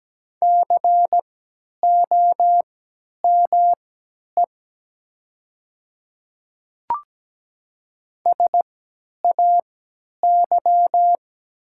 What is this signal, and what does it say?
Morse code, heard as a keyed tone, about 17 wpm, spaced as for 10 wpm, 700 Hz. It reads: COME SAY
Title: 3x Character Spacing - Top 200 Words - 17wpm